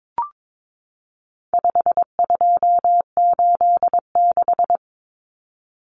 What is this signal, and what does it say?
Morse code, heard as a keyed tone, about 22 wpm, 700 Hz. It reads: 5286
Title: Numbers - 4-Digit 22wpm